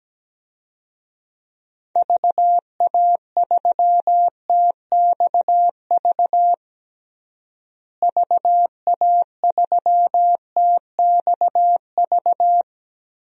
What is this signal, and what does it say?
Morse code, heard as a keyed tone, about 17 wpm, 700 Hz. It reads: VA3TXV VA3TXV
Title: DXCC-100 Call Signs - 6 Characters - Repeated 2x - 17wpm